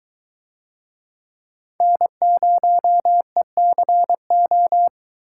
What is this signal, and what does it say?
Morse code, heard as a keyed tone, about 21 wpm, 700 Hz. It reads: N0ECO